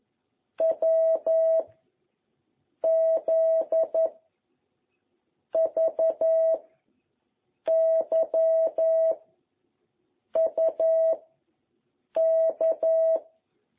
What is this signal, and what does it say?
Morse code, heard as a keyed tone, about 11 wpm, 660 Hz. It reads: W Z V Y U K